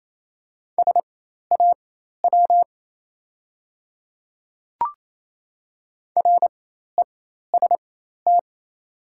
Morse code, heard as a keyed tone, about 28 wpm, spaced as for 13 wpm, 700 Hz. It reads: SAW REST
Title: 4x Character Spacing - Top 500 Words - 28wpm